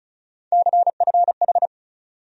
Morse code, heard as a keyed tone, about 36 wpm, 700 Hz. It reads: CFH
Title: USA Call Sign Postfixes - 3 Letters 35wpm